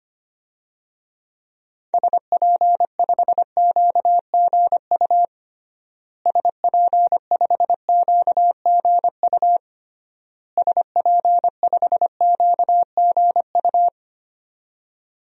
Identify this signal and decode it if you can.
Morse code, heard as a keyed tone, about 25 wpm, 700 Hz. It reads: SP5QGU SP5QGU SP5QGU
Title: DXCC-100 Call Signs - 6 Characters - Repeated 3x - 25wpm